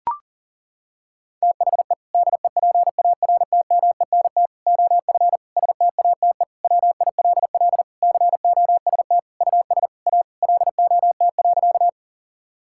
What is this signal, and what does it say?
Morse code, heard as a keyed tone, about 40 wpm, 700 Hz. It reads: THE DEPARTMENT OF STATE WILL COST US A LOT.
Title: Sentences from Top 1000 Words - Review Part 5 - 40wpm